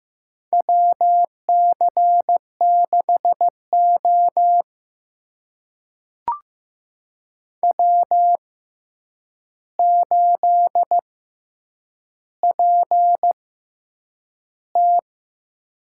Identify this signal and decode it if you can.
Morse code, heard as a keyed tone, about 15 wpm, spaced as for 8 wpm, 700 Hz. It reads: WC6O W8PT
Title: USA Call Signs - 4 Characters - 6x Character Spacing - 15wpm